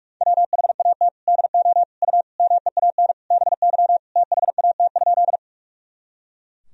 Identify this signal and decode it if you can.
Morse code, heard as a keyed tone, about 41 wpm, 700 Hz. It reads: WHAT DO U MEAN BY THAT?
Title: Sentences from Top 200 Words and Abbreviations 45wpm